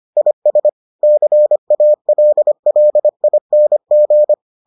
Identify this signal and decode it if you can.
Morse code, heard as a keyed tone, about 25 wpm, 600 Hz. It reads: IS CALLING